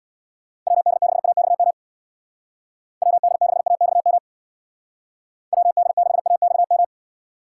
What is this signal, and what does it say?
Morse code, heard as a keyed tone, about 59 wpm, 700 Hz. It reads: WD6AXM WD6AXM WD6AXM
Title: Speed Racing - USA 2x3 Call Signs 50wpm